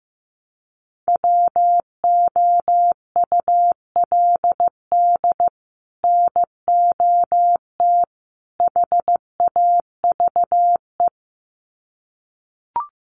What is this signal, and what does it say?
Morse code, heard as a keyed tone, about 15 wpm, 700 Hz. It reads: WOULD NOT HAVE
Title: Sets of 3 Words 15wpm